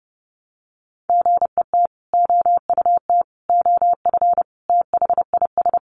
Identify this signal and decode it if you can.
Morse code, heard as a keyed tone, about 30 wpm, 700 Hz. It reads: GET OUT OF THIS
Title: Sentences from Top 1000 Words 30wpm